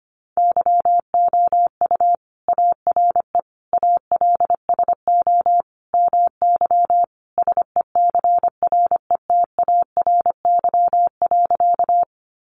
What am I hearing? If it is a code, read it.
Morse code, heard as a keyed tone, about 25 wpm, 700 Hz. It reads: YOU ARE ALSO MY SECRETARY.
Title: Sentences from Top 1300 Words 25wpm